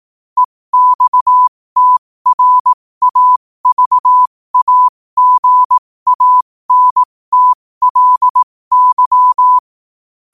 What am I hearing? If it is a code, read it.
Morse code, heard as a keyed tone, about 18 wpm, spaced as for 15 wpm, 1000 Hz. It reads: EXTRAVAGANTLY
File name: qotdfiles / 15wpm